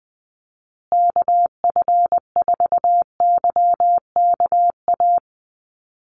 Morse code, heard as a keyed tone, about 20 wpm, 700 Hz. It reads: KF4YKA
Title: USA Call Signs Repeated 2x 20wpm